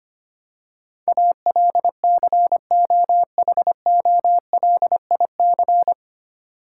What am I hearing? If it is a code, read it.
Morse code, heard as a keyed tone, about 25 wpm, 700 Hz. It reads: ALCOHOLIC
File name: Nine Letter Words - 25wpm